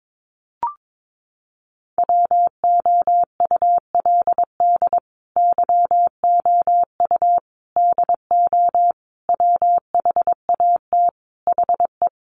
Morse code, heard as a keyed tone, about 22 wpm, 700 Hz. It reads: WOULD YOU DO WHAT HE
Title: Sentences from Top 100 Words 22wpm